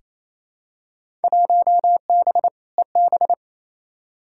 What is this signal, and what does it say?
Morse code, heard as a keyed tone, about 28 wpm, 700 Hz. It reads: 1B EB
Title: Field Day Exchanges Encore - 28wpm